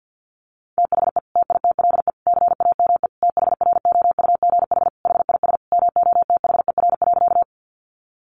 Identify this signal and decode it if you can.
Morse code, heard as a keyed tone, about 50 wpm, 700 Hz. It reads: THE TITLE CAME THROUGH HIS MOTHER.
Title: Sentences from Top 1200 Words 50wpm